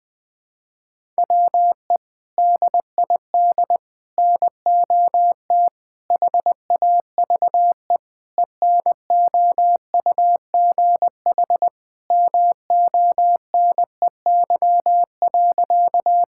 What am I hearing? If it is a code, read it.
Morse code, heard as a keyed tone, about 20 wpm, 700 Hz. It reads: WE DID NOT HAVE ENOUGH MONEY.
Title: Sentences with 4-Word Phrases - 20wpm